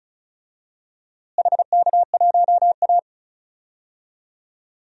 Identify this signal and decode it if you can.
Morse code, heard as a keyed tone, about 36 wpm, 700 Hz. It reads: HK1A
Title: DXCC-100 Call Signs - 4 Characters - Repeated 2x - 35wpm